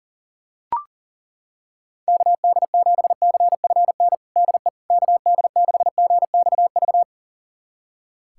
Morse code, heard as a keyed tone, about 40 wpm, 700 Hz. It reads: KD7CFN DE KD6GXV
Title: USA Callsign DE Callsign 40wpm